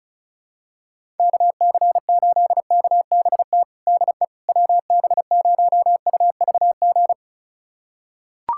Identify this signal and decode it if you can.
Morse code, heard as a keyed tone, about 36 wpm, 700 Hz. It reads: KC8KBT DE WB0UVG
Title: USA Callsign DE Callsign - 6 Characters - 35wpm